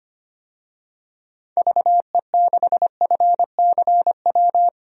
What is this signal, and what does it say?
Morse code, heard as a keyed tone, about 25 wpm, 700 Hz. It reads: VE6FCW
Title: DXCC-100 Call Signs - 6 Characters - 25wpm